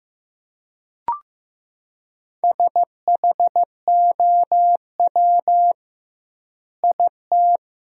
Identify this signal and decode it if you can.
Morse code, heard as a keyed tone, about 15 wpm, 700 Hz. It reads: SHOW IT
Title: 2x Word-Spacing - Sentences from Top 200 Words - 15wpm